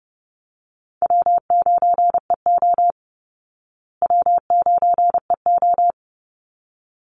Morse code, heard as a keyed tone, about 30 wpm, 700 Hz. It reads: W9EO W9EO
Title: Call Signs Repeated 3x - 30wpm